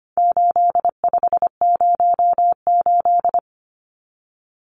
Morse code, heard as a keyed tone, about 25 wpm, 700 Hz. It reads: 8508
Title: Numbers - 4-Digit 25wpm